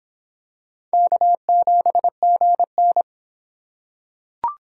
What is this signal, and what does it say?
Morse code, heard as a keyed tone, about 23 wpm, 700 Hz. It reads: K7GN